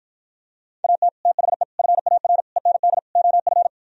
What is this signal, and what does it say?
Morse code, heard as a keyed tone, about 52 wpm, 700 Hz. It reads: AT THE FAR END OF